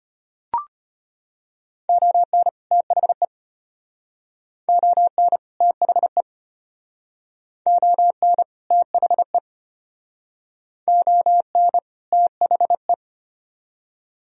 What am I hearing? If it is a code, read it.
Morse code, heard as a keyed tone, about 32 wpm, 700 Hz. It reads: ON THE ON THE ON THE ON THE